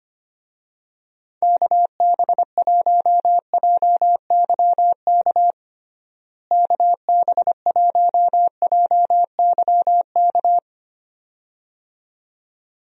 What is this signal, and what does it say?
Morse code, heard as a keyed tone, about 25 wpm, 700 Hz. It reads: KB1JYK KB1JYK
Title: USA Call Signs - 6 Characters - Repeated 2x - 25wpm